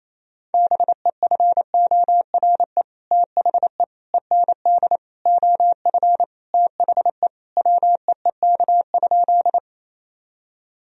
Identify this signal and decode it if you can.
Morse code, heard as a keyed tone, about 28 wpm, 700 Hz. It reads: BEFORE THE END OF THE WEEK?
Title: Sentences from Top 1000 Words - Review Part 2 - 28wpm